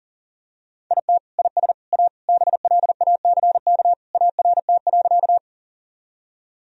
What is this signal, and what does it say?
Morse code, heard as a keyed tone, about 40 wpm, 700 Hz. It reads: IT IS A BLACK ART.
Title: Sentences from Top 1000 Words - Review Part 3 - 40wpm